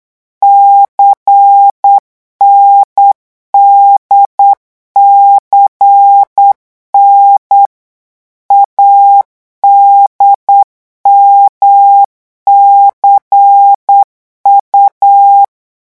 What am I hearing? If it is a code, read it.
Morse code, heard as a keyed tone, about 8 wpm, 800 Hz. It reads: CNDCN ADMCU
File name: morse mp3 / revision3-vitesse 8 mots